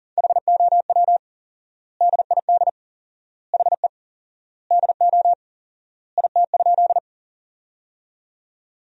Morse code, heard as a keyed tone, about 40 wpm, 700 Hz. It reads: HOW DID HE DO IT?
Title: 4x Word-Spacing - Sentences from Top 100 Words - 40wpm